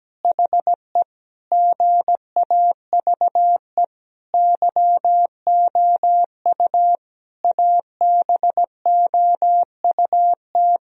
Morse code, heard as a keyed tone, about 17 wpm, 700 Hz. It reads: HE GAVE YOU ABOUT
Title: Sentences from Top 1000 Words - Review Part 3 - 17wpm